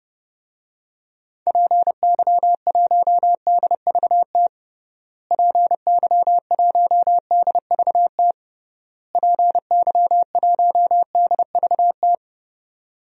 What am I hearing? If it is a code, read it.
Morse code, heard as a keyed tone, about 30 wpm, 700 Hz. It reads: PY1DVT PY1DVT PY1DVT
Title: DXCC-100 Call Signs - 6 Characters - Repeated 3x - 30wpm